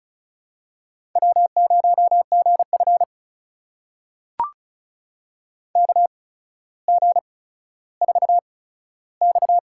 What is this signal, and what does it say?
Morse code, heard as a keyed tone, about 36 wpm, spaced as for 15 wpm, 700 Hz. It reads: W0GF KG4X
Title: USA Call Signs - 4 Characters - 8x Character Spacing - 35wpm